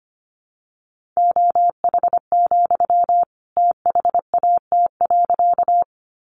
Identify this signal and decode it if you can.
Morse code, heard as a keyed tone, about 25 wpm, 700 Hz. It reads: OH, THAT.
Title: Sentences from Top 500 Words 25wpm